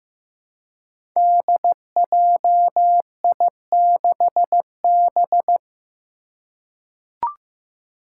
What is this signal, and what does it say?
Morse code, heard as a keyed tone, about 15 wpm, 700 Hz. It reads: DJI6B
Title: Five Challenging Letter-Number Combinations 15wpm